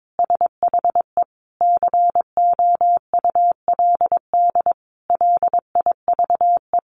Morse code, heard as a keyed tone, about 22 wpm, 700 Hz. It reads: SHE COULD LIVE
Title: Sentences from Top 1400 Words 22wpm